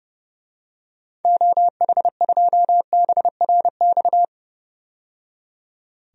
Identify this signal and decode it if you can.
Morse code, heard as a keyed tone, about 31 wpm, 700 Hz. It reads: OH2BRX
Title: DXCC-100 Call Signs - 6 Characters - 30wpm